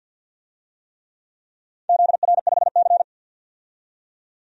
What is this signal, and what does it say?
Morse code, heard as a keyed tone, about 52 wpm, 700 Hz. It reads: ZR5C
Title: DXCC-100 Call Signs - 4 Characters - 50wpm